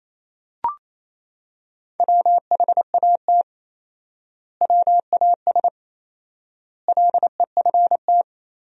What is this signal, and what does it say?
Morse code, heard as a keyed tone, about 28 wpm, 700 Hz. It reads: WHAT WAS LEFT
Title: 4x Word-Spacing - Sentences from Top 300 Words - 28wpm